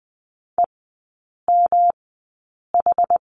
Morse code, heard as a keyed tone, about 20 wpm, spaced as for 8 wpm, 700 Hz. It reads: EMH
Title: 172 Lesson 28e - Number 9 - Three Characters 20wpm